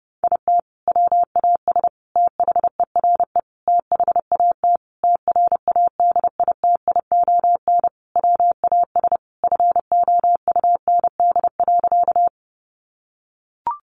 Morse code, heard as a keyed tone, about 30 wpm, 700 Hz. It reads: IT WAS THERE THAT TRADITION WAS FOUND.
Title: Sentences from Top 1500 Words 30wpm